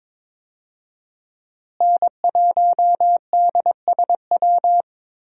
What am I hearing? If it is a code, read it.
Morse code, heard as a keyed tone, about 22 wpm, 700 Hz. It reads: N1DSW
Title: USA Call Signs - 5 Characters - Repeated 3x - 22wpm